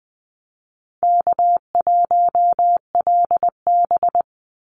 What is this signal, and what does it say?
Morse code, heard as a keyed tone, about 20 wpm, 700 Hz. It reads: K1LB
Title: NA SOTA Chaser Callsigns 20wpm